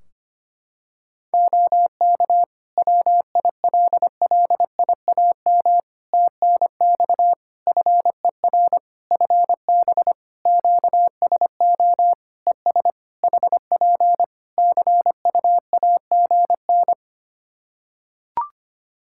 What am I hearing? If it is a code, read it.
Morse code, heard as a keyed tone, about 25 wpm, 700 Hz. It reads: OK WILLIAM TNX FER FB QSO ES HP CUAGN
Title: Practice QSO - Exchange 4 - Ending - 25wpm